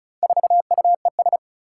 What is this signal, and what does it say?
Morse code, heard as a keyed tone, about 36 wpm, 700 Hz. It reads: 4UES
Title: Four Challenging Letter-Number Combinations 35wpm